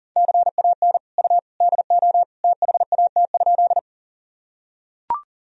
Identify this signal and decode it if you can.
Morse code, heard as a keyed tone, about 40 wpm, 700 Hz. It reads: CAN U DO THAT?
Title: Sentences from Top 100 Words and Abbreviations 40wpm